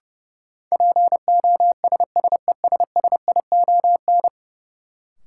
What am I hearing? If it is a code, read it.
Morse code, heard as a keyed tone, about 32 wpm, 700 Hz. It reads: POSSESSION
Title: Sentences from Top 1400 Words - New Word Drill - 30wpm